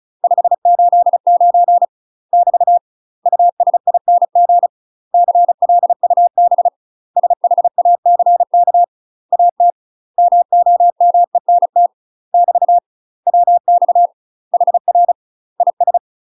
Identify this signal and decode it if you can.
Morse code, heard as a keyed tone, about 36 wpm, 700 Hz. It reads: 589 <BT> USING CLUB SHACK AT MOMENT <BT> WX HR IS
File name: UK MASTER QSOs TRAINING / 35WPM / UK MASTER QSOs 0029